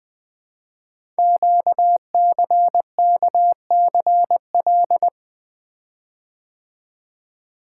Morse code, heard as a keyed tone, about 20 wpm, 700 Hz. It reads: QCKCL
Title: Code Groups - 5 Letters - 20wpm